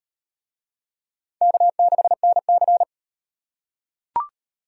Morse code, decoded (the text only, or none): K6NC